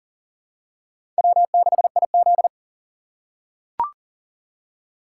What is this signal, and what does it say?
Morse code, heard as a keyed tone, about 40 wpm, 700 Hz. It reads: W6IZ
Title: USA Call Signs - 4 Characters - 6x Character Spacing - 40wpm